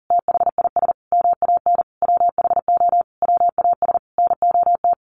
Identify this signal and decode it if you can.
Morse code, heard as a keyed tone, about 40 wpm, 700 Hz. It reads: THIS MAN WHO WAS NOT
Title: A Sound Of Thunder 40wpm